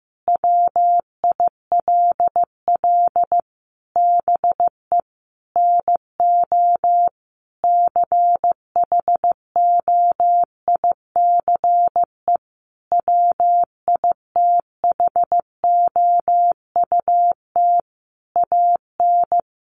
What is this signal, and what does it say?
Morse code, heard as a keyed tone, about 15 wpm, 700 Hz. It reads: WILL BE NO CHOICE WITHOUT AN
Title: Sentences from Top 1500 Words 15wpm